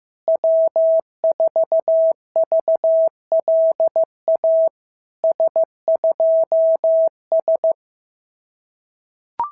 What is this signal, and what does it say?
Morse code, heard as a keyed tone, about 14 wpm, 650 Hz. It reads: W4VLA S2S
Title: SOTA QSO Practice 15wpm